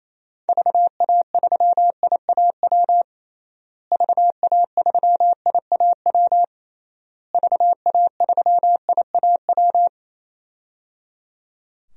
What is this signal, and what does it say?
Morse code, heard as a keyed tone, about 28 wpm, 700 Hz. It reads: VA3IAW VA3IAW VA3IAW
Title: DXCC-100 Call Signs - 6 Characters - Repeated 3x - 28wpm